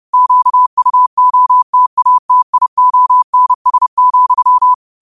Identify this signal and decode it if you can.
Morse code, heard as a keyed tone, about 30 wpm, 1000 Hz. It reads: QUOTATIONS,